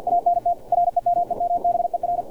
Morse code, heard as a keyed tone, about 50 wpm, 700 Hz, with radio noise. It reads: ANT REMEMBER
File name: archived / CRSnet